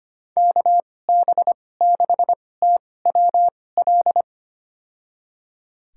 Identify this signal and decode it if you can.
Morse code, heard as a keyed tone, about 25 wpm, spaced as for 18 wpm, 700 Hz. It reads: KB6TWL
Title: USA Call Signs - 6 Characters - 2x Character Spacing - 25wpm